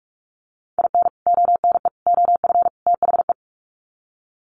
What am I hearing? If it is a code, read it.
Morse code, heard as a keyed tone, about 44 wpm, 700 Hz. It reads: IN ONE OF THE
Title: Sets of 4 Words 45wpm